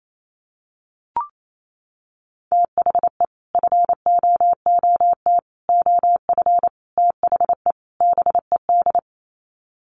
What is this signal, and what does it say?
Morse code, heard as a keyed tone, about 28 wpm, 700 Hz. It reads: THE FOOT OF THE BED
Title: Sets of 5 Words Encore2 28wpm